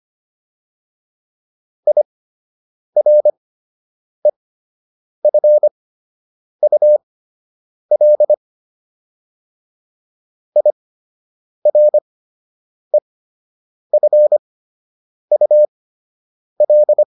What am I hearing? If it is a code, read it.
Morse code, heard as a keyed tone, about 25 wpm, spaced as for 8 wpm, 600 Hz. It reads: IREFUL IREFUL